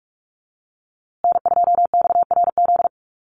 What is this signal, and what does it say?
Morse code, heard as a keyed tone, about 44 wpm, 700 Hz. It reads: N2XRZ
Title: USA Call Signs Repeated 3x 45wpm